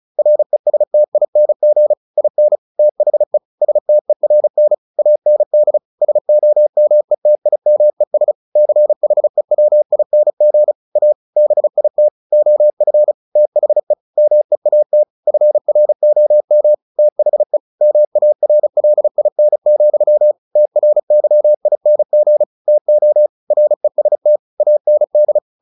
Morse code, heard as a keyed tone, about 36 wpm, 600 Hz. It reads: RESTING IN THE STERN AND SOMETIMES CHEWING A BIT OF THE MEAT FROM THE MARLIN, TRYING TO REST AND
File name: the old man and the sea 2 35wpm